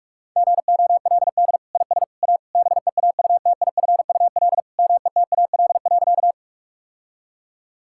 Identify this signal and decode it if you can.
Morse code, heard as a keyed tone, about 41 wpm, 700 Hz. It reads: GOLD IS A BEAUTIFUL METAL.
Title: Sentences from Top 1000 Words - Review Part 6 - 45wpm